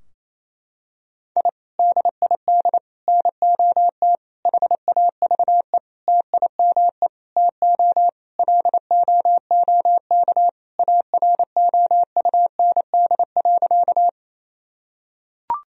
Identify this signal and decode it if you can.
Morse code, heard as a keyed tone, about 28 wpm, 700 Hz. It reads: I DID NOT HAVE TIME TO LOOK AROUND.
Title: Sentences from Top 1000 Words - Review Part 3 - 28wpm